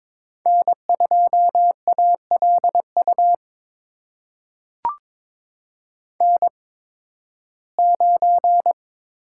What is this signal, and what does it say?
Morse code, heard as a keyed tone, about 22 wpm, spaced as for 14 wpm, 700 Hz. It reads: N2ALU N9
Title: USA Call Signs - 5 Characters - 8x Character Spacing - 22wpm